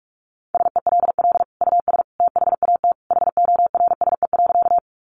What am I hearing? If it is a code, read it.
Morse code, heard as a keyed tone, about 44 wpm, 700 Hz. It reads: SELL US THAT HORSE.
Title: Sentences from Top 1100 Words 45wpm